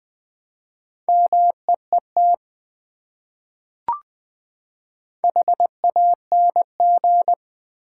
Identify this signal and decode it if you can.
Morse code, heard as a keyed tone, about 20 wpm, 700 Hz. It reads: MEET HANG
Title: Four Letter Words - 20wpm